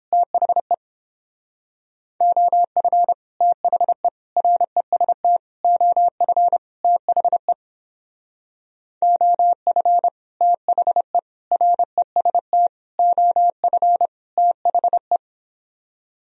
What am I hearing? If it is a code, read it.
Morse code, heard as a keyed tone, about 29 wpm, 700 Hz. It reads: THE OF THE REST OF THE OF THE REST OF THE